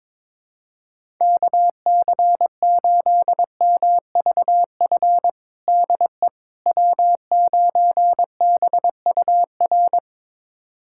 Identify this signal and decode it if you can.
Morse code, heard as a keyed tone, about 22 wpm, 700 Hz. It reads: KC8MVF DE W9BUR
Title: USA Callsign DE Callsign 22wpm